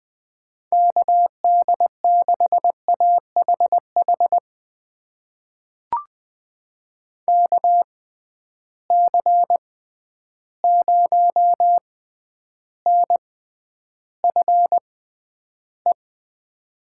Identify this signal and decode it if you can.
Morse code, heard as a keyed tone, about 20 wpm, spaced as for 10 wpm, 700 Hz. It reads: KD6AHH KC0NFE
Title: USA Call Signs - 6x Character Spacing - 20wpm